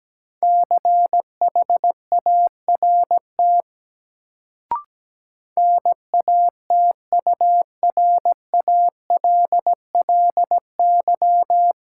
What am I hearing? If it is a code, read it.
Morse code, heard as a keyed tone, about 17 wpm, 700 Hz. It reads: CHART NATURALLY
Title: Sentences from Top 1300 Words - New Word Drill Alt - 17wpm